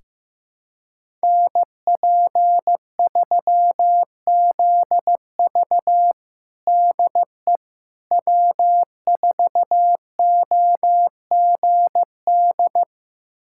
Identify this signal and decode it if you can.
Morse code, heard as a keyed tone, about 16 wpm, 700 Hz. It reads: NP3ZV DE W4OGD
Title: USA Callsign DE Callsign - 5 Characters - 15wpm